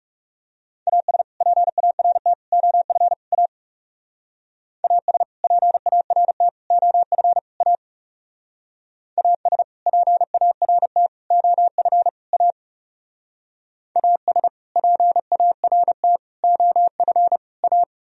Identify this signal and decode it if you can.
Morse code, heard as a keyed tone, about 36 wpm, 700 Hz. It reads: AS PART OF A AS PART OF A AS PART OF A AS PART OF A
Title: Speed-Racing - Sets of 4 Words 30wpm